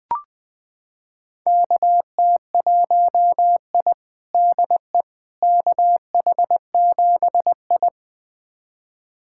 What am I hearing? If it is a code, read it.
Morse code, heard as a keyed tone, about 20 wpm, 700 Hz. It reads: KT1I DE KH7I
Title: USA Callsign DE Callsign - 4 Characters - 20wpm